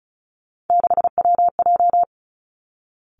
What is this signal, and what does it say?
Morse code, heard as a keyed tone, about 36 wpm, 700 Hz. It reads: 6WJ
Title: Three Challenging Letter-Number Combinations 35wpm